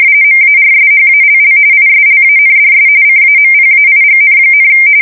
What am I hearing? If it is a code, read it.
RTTY, about 45.45 baud. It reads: MARSHALL ISLANDS CAMPAIGN,